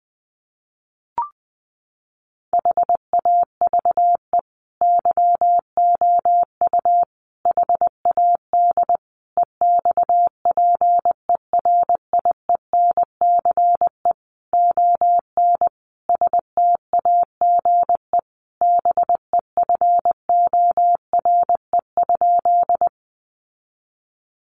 Morse code, heard as a keyed tone, about 20 wpm, 700 Hz. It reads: HAVE YOU HAD EXPERIENCE ON STAGE BEFORE?
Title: Sentences from Top 1200 Words 20wpm